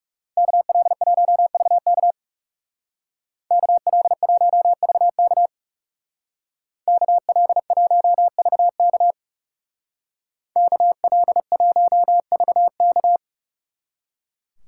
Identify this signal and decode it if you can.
Morse code, heard as a keyed tone, about 40 wpm, 700 Hz. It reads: KL1VK KL1VK KL1VK KL1VK